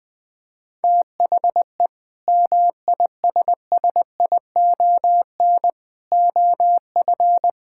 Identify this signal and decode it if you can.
Morse code, heard as a keyed tone, about 20 wpm, 700 Hz. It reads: THE MISSION OF
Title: Sentences from Top 1600 Words 20wpm